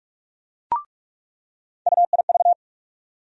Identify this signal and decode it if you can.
Morse code, heard as a keyed tone, about 46 wpm, 700 Hz. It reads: UIV
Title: USA Call Sign Postfixes - 3 Letters 45wpm